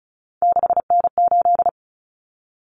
Morse code, heard as a keyed tone, about 34 wpm, 700 Hz. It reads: 6N8
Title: Common Cut Numbers - 3-Digit 35wpm